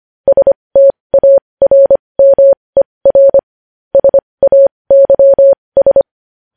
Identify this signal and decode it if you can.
Morse code, heard as a keyed tone, about 25 wpm, spaced as for 20 wpm, 550 Hz. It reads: STARMER SAYS
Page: news-headlines-in-morse-code-at-20-wpm